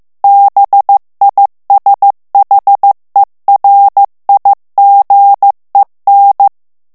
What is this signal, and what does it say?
Morse code, heard as a keyed tone, about 15 wpm, 800 Hz. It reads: BISHERIGEN
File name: morse mp3 / ALLEMAND texte clair 1 vitesse 14 mots minute